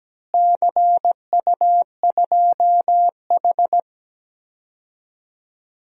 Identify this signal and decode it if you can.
Morse code, heard as a keyed tone, about 17 wpm, 700 Hz. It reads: CU2H
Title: DXCC-100 Call Signs - 4 Characters - Repeated 3x - 17wpm